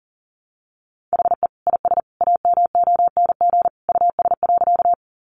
Morse code, heard as a keyed tone, about 40 wpm, 700 Hz. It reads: HE IS AMONG US.
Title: Sentences from Top 500 Words 40wpm